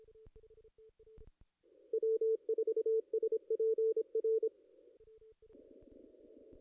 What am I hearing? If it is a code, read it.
Morse code, heard as a keyed tone, about 24 wpm, 440 Hz. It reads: W4SPR
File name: contests / ss-cw-k5tr-op-n5rz -2300